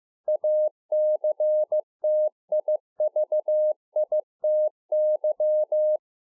Morse code, heard as a keyed tone, about 15 wpm, 610 Hz. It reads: ACTIVITY
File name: 15WPM